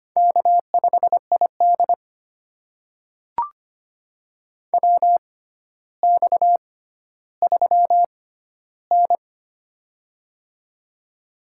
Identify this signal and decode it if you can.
Morse code, heard as a keyed tone, about 25 wpm, spaced as for 13 wpm, 700 Hz. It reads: K5ID WX3N
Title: USA Call Signs - 4 Characters - 6x Character Spacing - 25wpm